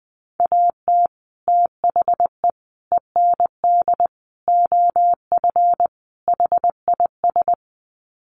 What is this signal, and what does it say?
Morse code, heard as a keyed tone, about 20 wpm, 700 Hz. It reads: AT THE END OF HIS
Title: Sets of 5 Words Encore 20wpm